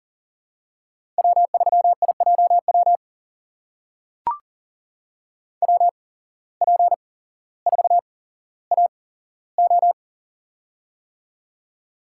Morse code, heard as a keyed tone, about 40 wpm, spaced as for 17 wpm, 690 Hz. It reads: W3IJW WP4AO
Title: USA Call Signs - 8x Character Spacing - 40wpm